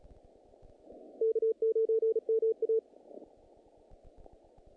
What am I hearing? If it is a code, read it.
Morse code, heard as a keyed tone, about 36 wpm, 450 Hz. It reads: K9MA
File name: contests / naqp-cw-k5tr-op-k5pi- -2100